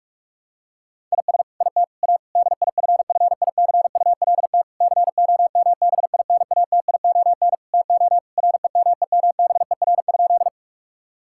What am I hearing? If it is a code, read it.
Morse code, heard as a keyed tone, about 46 wpm, 700 Hz. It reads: IS IT A DIFFICULT COMBINATION TO REMEMBER?
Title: Sentences from Top 1000 Words - Review Part 6 - 45wpm